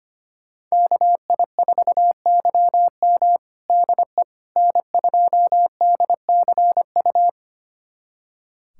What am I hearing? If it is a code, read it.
Morse code, heard as a keyed tone, about 25 wpm, 700 Hz. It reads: KI4YM DE N2DCU
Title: USA Callsign DE Callsign - 5 Characters - 25wpm